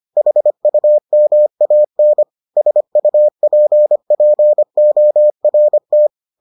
Morse code, heard as a keyed tone, about 25 wpm, 600 Hz. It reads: HUMAN SUPPORT